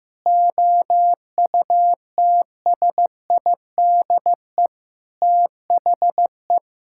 Morse code, heard as a keyed tone, about 15 wpm, 700 Hz. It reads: OUTSIDE THE
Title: Sentences from Top 1000 Words - Review Part 6 - 15wpm